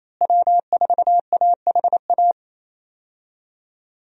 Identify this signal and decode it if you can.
Morse code, heard as a keyed tone, about 27 wpm, 700 Hz. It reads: W4AHA
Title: USA Call Signs - 5 Characters - Repeated 3x - 28wpm